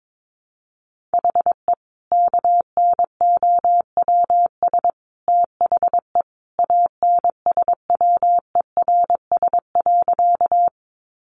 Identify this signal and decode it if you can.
Morse code, heard as a keyed tone, about 22 wpm, 700 Hz. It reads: HE KNOWS THE ANSWERS.
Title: Additional Easy Phrases 22wpm